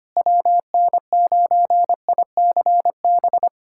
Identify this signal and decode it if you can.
Morse code, heard as a keyed tone, about 25 wpm, 700 Hz. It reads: WN9ICB
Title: USA Callsign DE Callsign 25wpm